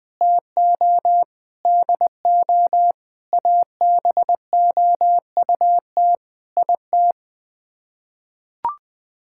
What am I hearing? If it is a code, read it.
Morse code, heard as a keyed tone, about 20 wpm, 700 Hz. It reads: TO DO ABOUT IT
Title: Sets of 8 Words - 20wpm